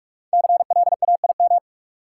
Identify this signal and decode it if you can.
Morse code, heard as a keyed tone, about 41 wpm, 700 Hz. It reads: CLAIM
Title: Sentences from Top 1500 Words - New Word Drill Alt - 45wpm